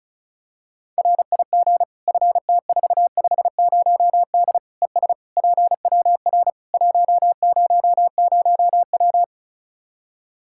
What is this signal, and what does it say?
Morse code, heard as a keyed tone, about 34 wpm, 700 Hz. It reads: RIG FT450D ES PWR 100W
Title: K1USN SST QSO Practice 35wpm